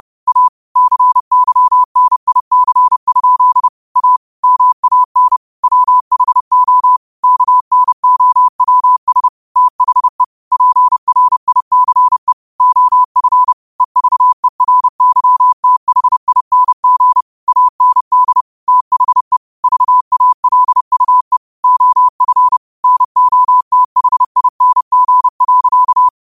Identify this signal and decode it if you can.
Morse code, heard as a keyed tone, about 30 wpm, 1000 Hz. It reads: A CYNIC? A MAN WHO KNOWS THE PRICE OF EVERYTHING AND THE VALUE OF NOTHING.
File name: qotdfiles / 30wpm